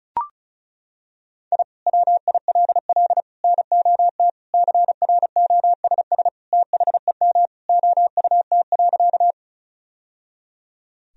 Morse code, heard as a keyed tone, about 34 wpm, 700 Hz. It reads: I WILL NOT CROSS THEM OUT.
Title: Sentences from Top 1000 Words - Review Part 1 - 35wpm